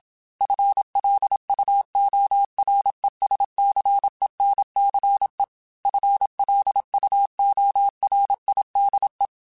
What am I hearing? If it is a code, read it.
Morse code, heard as a keyed tone, about 27 wpm, 800 Hz. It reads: FLUORESCENCE FLUORIDE